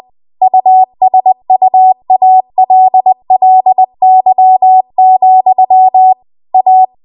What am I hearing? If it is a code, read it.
Morse code, heard as a keyed tone, about 20 wpm, 750 Hz. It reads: USUALLY, A
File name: Archive / 20 WPM / 20WPM